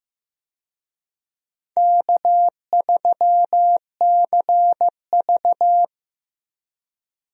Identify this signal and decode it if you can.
Morse code, heard as a keyed tone, about 15 wpm, 700 Hz. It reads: K3CV
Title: USA Call Signs - 4 Characters - 15wpm